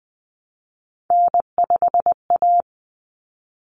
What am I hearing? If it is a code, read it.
Morse code, heard as a keyed tone, about 20 wpm, 700 Hz. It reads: N5A
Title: Common Cut Numbers - 3-Digit 20wpm